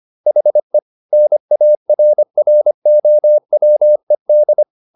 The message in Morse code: HE NARROWED

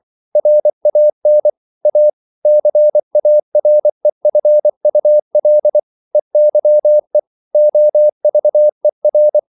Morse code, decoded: RAN A CAREFUL EYE OVER

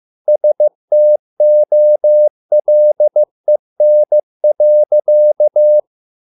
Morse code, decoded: STOLEN.

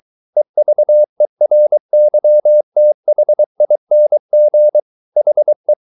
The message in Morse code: EVERYTHING HE

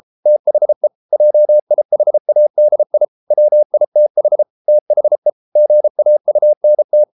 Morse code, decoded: THE JIHADI WITH THE GAUNT

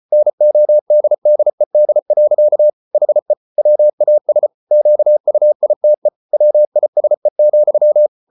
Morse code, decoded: NODDED. HE WAS QUITE WISE,

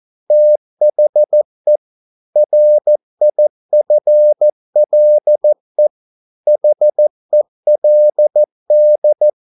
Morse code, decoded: THE RIFLE HELD

